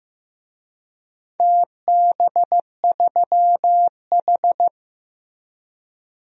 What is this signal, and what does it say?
Morse code, heard as a keyed tone, about 15 wpm, 700 Hz. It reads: TB3H
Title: DXCC-100 Call Signs - 4 Characters - Repeated 3x - 15wpm